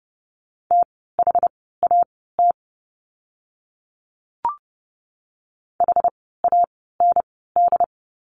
Morse code, heard as a keyed tone, about 30 wpm, spaced as for 17 wpm, 700 Hz. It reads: THAT HAND
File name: course / 30:10wpm / 115 Lesson 19f - Letter w - Words 30wpm